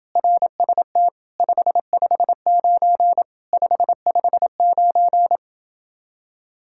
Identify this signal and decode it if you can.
Morse code, heard as a keyed tone, about 29 wpm, 700 Hz. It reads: RST 559 559